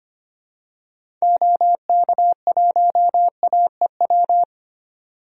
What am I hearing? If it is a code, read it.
Morse code, heard as a keyed tone, about 25 wpm, 700 Hz. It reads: OK1AEW
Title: DXCC-100 Call Signs - 6 Characters - 25wpm